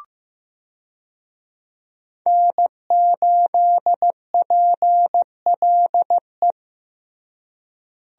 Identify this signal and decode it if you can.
Morse code, heard as a keyed tone, about 16 wpm, 700 Hz. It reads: N8PLE
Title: USA Call Signs - 5 Characters - Repeated 3x - 15wpm